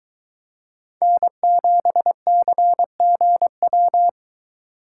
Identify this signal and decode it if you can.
Morse code, heard as a keyed tone, about 22 wpm, 700 Hz. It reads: N7CGW